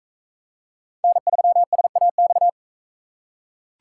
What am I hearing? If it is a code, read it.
Morse code, heard as a keyed tone, about 39 wpm, 700 Hz. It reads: N3SAX